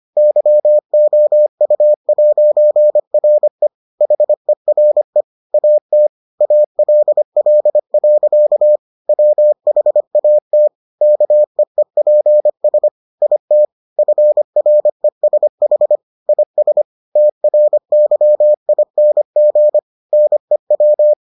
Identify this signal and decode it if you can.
Morse code, heard as a keyed tone, about 25 wpm, 600 Hz. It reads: YOU'RE HERE AT ALL. WHAT KEEPS IT FRESH IS TRYING NEW